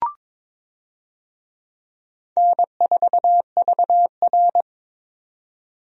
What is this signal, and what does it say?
Morse code, heard as a keyed tone, about 22 wpm, 700 Hz. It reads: N4VR
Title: USA Call Signs - 4 Characters - 22wpm